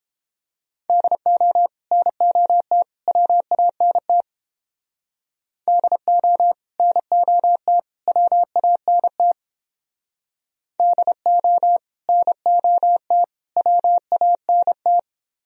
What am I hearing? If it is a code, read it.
Morse code, heard as a keyed tone, about 29 wpm, 700 Hz. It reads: DO NOT WANT DO NOT WANT DO NOT WANT